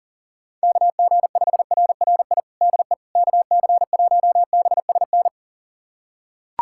Morse code, heard as a keyed tone, about 37 wpm, 700 Hz. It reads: KG5RRI DE KC1BSN
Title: USA Callsign DE Callsign 40wpm